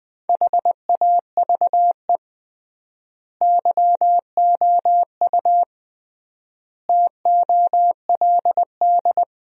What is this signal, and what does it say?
Morse code, heard as a keyed tone, about 21 wpm, 700 Hz. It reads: HAVE YOU TOLD
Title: 3x Word-Spacing - Sentences from Top 400 Words - 20wpm